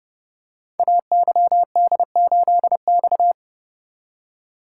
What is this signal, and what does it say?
Morse code, heard as a keyed tone, about 30 wpm, 700 Hz. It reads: AYD8X
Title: Code Groups - 5 Letter:Numbers - 30wpm